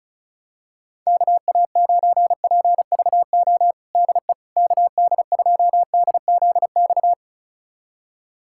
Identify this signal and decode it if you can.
Morse code, heard as a keyed tone, about 36 wpm, 700 Hz. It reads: KA9PVO DE KD2DZX
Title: USA Callsign DE Callsign - 6 Characters - 35wpm